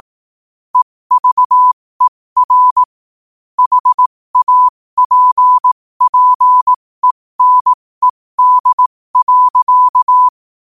Morse code, heard as a keyed tone, about 18 wpm, spaced as for 15 wpm, 1000 Hz. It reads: EVER HAPPENED.
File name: qotdfiles / 15wpm